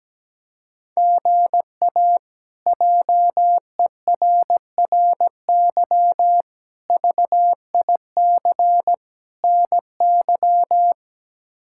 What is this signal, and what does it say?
Morse code, heard as a keyed tone, about 17 wpm, 700 Hz. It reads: GA JERRY VIC NY
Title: K1USN SST QSO Practice - 17wpm